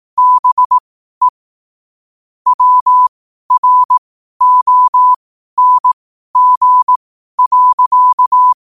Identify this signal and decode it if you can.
Morse code, heard as a keyed tone, about 18 wpm, spaced as for 12 wpm, 1000 Hz. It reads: BE WRONG.
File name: qotdfiles / 12wpm